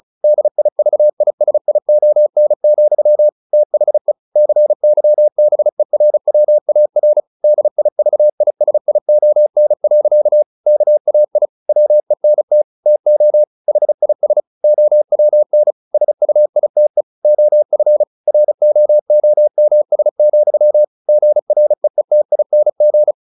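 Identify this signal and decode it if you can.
Morse code, heard as a keyed tone, about 34 wpm, 600 Hz. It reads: DIVISION, THE CYBERWAR DIVISION. KAI WENT TO HIS OWN SUITE OF ROOMS, GREETING
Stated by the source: Morse practice at 35 wpm from CW med Gnister, sending Ken Follett's novel Never in English.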